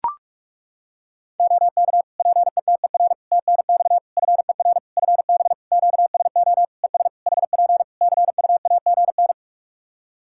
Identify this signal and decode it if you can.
Morse code, heard as a keyed tone, about 41 wpm, 700 Hz. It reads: OK PETER TNX FER FB QSO ES HP CUAGN
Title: Practice QSO - Exchange 4 - Ending - 45wpm